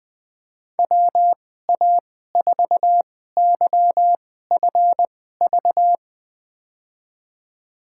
Morse code, heard as a keyed tone, about 20 wpm, spaced as for 14 wpm, 700 Hz. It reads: WA4YFV
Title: USA Call Signs - 2x Character Spacing - 20wpm